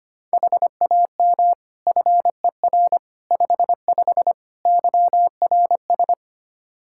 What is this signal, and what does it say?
Morse code, heard as a keyed tone, about 25 wpm, 700 Hz. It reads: HAM FER 55 YRS
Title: Practice QSO - Exchange 3 - More Info - 25wpm